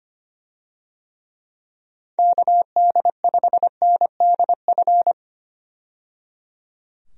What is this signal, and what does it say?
Morse code, heard as a keyed tone, about 25 wpm, 700 Hz. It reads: KD5NDF
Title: USA Call Signs - 6 Characters - 25wpm